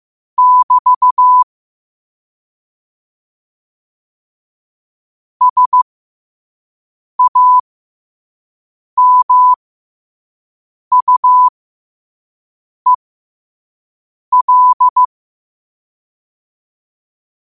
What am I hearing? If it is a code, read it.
Morse code, heard as a keyed tone, about 15 wpm, spaced as for 5 wpm, 1000 Hz. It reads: <BT> SAMUEL
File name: qotdfiles / 5wpm